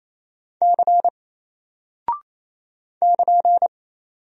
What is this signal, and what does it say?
Morse code, heard as a keyed tone, about 27 wpm, 700 Hz. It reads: C <KN>
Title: Single Character (Letter:Number:Punctuation:Prosign) Alt 28wpm